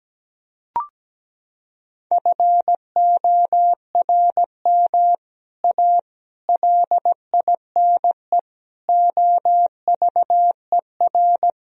Morse code, heard as a keyed tone, about 17 wpm, 700 Hz. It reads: FORM A LINE OVER
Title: Sentences from Top 1000 Words - Review Part 1 - 17wpm